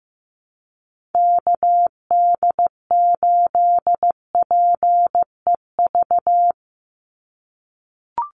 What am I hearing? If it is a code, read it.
Morse code, heard as a keyed tone, about 15 wpm, 700 Hz. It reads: KD8PEV
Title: USA Call Signs 15wpm